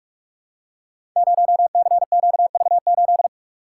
Morse code, heard as a keyed tone, about 44 wpm, 700 Hz. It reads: 0CQV8
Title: Code Groups - 5 Letter:Numbers - 45wpm